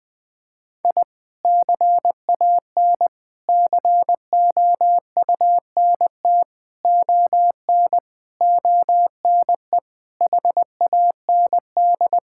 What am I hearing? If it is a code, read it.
Morse code, heard as a keyed tone, about 21 wpm, 700 Hz. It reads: I CAN COUNT ON ONE HAND
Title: Sentences with 8-Word Phrases - 20wpm